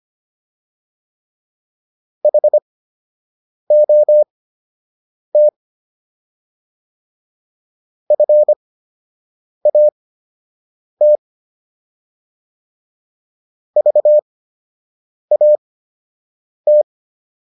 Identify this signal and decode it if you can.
Morse code, heard as a keyed tone, about 25 wpm, spaced as for 7 wpm, 600 Hz. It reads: HOT FAT VAT